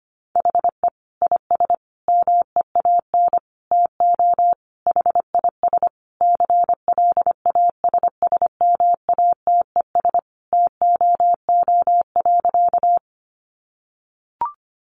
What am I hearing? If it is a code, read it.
Morse code, heard as a keyed tone, about 25 wpm, 700 Hz. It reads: HE IS MEAN TO HIS CLASSMATES TOO.
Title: Additional Easy Phrases 25wpm